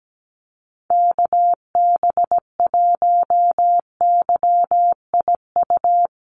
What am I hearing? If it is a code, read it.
Morse code, heard as a keyed tone, about 17 wpm, 700 Hz. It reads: KB1YIU
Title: USA Call Signs 17wpm